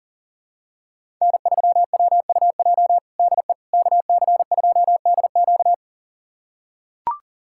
Morse code, heard as a keyed tone, about 40 wpm, 700 Hz. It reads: N3WUJ DE KC2DQ
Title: USA Callsign DE Callsign - 5 Characters - 40wpm